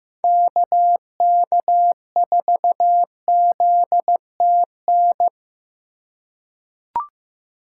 Morse code, heard as a keyed tone, about 15 wpm, 700 Hz. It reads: KK4ZTN
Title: USA Call Signs - 4x Character Spacing - 15wpm